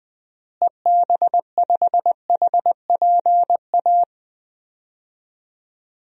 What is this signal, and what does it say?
Morse code, heard as a keyed tone, about 20 wpm, 700 Hz. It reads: EB5HPA
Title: DXCC-100 Call Signs - 6 Characters - Repeated 3x - 20wpm